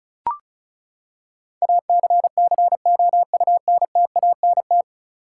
Morse code, heard as a keyed tone, about 36 wpm, 700 Hz. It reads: ACCOUNTANT
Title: Practice QSO - Exchange 3 - More Info - 35wpm